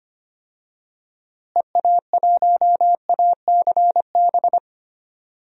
Morse code, heard as a keyed tone, about 25 wpm, 700 Hz. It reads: EA1ACB
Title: DXCC-100 Call Signs - 6 Characters - Repeated 3x - 25wpm